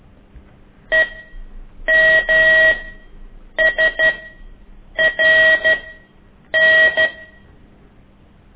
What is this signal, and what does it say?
Morse code, heard as a keyed tone, about 12 wpm, spaced as for 7 wpm, 1900 Hz. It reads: EMSRN
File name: ik1weg2 / CORSI CW